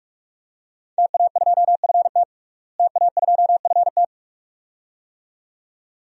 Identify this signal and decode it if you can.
Morse code, heard as a keyed tone, about 46 wpm, 700 Hz. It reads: TA2FT TA2FT
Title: DXCC-100 Call Signs - 5 Characters - Repeated 2x - 45wpm